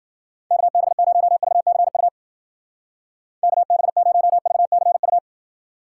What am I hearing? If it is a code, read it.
Morse code, heard as a keyed tone, about 50 wpm, 700 Hz. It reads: KB0VCU KB0VCU